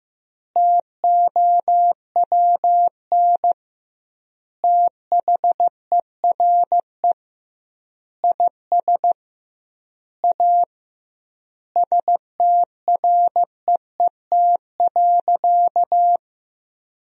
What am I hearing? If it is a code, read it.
Morse code, heard as a keyed tone, about 16 wpm, 700 Hz. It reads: TOWN THERE IS A STREET.
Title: 2x Word-Spacing - Sentences from Top 500 Words - 15wpm